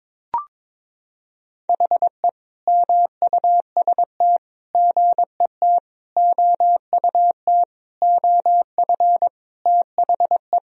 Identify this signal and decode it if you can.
Morse code, heard as a keyed tone, about 22 wpm, 700 Hz. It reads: HE MUST GET OUT OF THE
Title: Sentences from Top 1000 Words - Review Part 1 - 22wpm